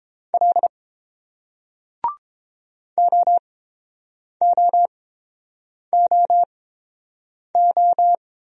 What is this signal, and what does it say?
Morse code, heard as a keyed tone, about 30 wpm, spaced as for 7 wpm, 700 Hz. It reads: L OOOO